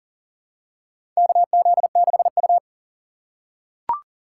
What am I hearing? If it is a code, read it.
Morse code, heard as a keyed tone, about 38 wpm, 700 Hz. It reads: KZ6U
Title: USA Call Signs - 4 Characters - 2x Character Spacing - 40wpm